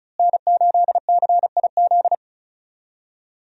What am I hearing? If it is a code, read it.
Morse code, heard as a keyed tone, about 35 wpm, 700 Hz. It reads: N8CIZ
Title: USA Call Signs - 5 Characters - Repeated 2x - 35wpm